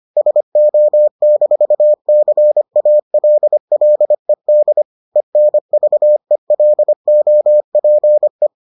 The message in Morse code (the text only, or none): SO-CALLED ENVELOPE